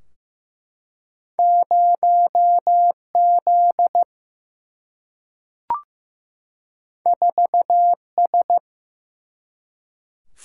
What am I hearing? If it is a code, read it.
Morse code, heard as a keyed tone, about 16 wpm, 700 Hz. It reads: 0 Z 4 S
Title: Single-Letter Single-Number 15wpm